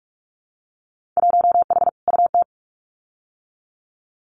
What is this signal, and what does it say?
Morse code, heard as a keyed tone, about 46 wpm, 700 Hz. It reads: 1H UT
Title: Winter Field Day Exchanges Encore 45wpm